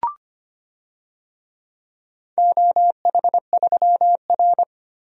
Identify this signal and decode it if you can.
Morse code, heard as a keyed tone, about 25 wpm, 700 Hz. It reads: OH3R
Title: DXCC-100 Call Signs - 4 Characters - 25wpm